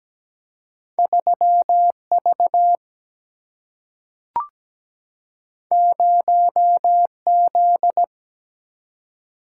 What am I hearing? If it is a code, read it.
Morse code, heard as a keyed tone, about 17 wpm, 700 Hz. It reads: 3 V 0 Z
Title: Single-Letter Single-Number 17wpm